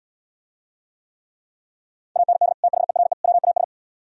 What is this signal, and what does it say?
Morse code, heard as a keyed tone, about 77 wpm, 700 Hz. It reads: AND THERE WAS